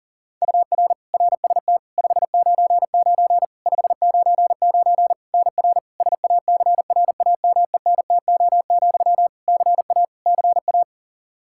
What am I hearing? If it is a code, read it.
Morse code, heard as a keyed tone, about 40 wpm, 700 Hz. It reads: UR RST 599 599 NR SACRAMENTO, CA CA
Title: K1USN SST QSO Practice 40wpm